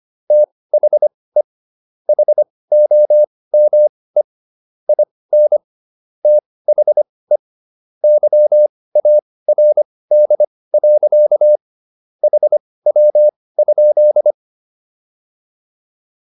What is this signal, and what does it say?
Morse code, heard as a keyed tone, about 25 wpm, spaced as for 18 wpm, 600 Hz. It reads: THE HOME IN THE YARD. HW?